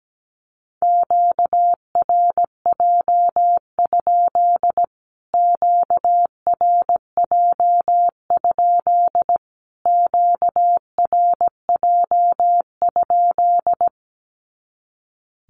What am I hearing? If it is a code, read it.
Morse code, heard as a keyed tone, about 17 wpm, 700 Hz. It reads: QRJ? QRJ? QRJ?
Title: Q-Codes Repeated 3x 17wpm